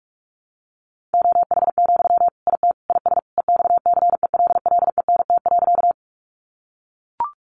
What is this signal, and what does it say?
Morse code, heard as a keyed tone, about 44 wpm, 700 Hz. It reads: OH, IT IS EXCELLENT.
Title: Sentences from Top 1000 Words 45wpm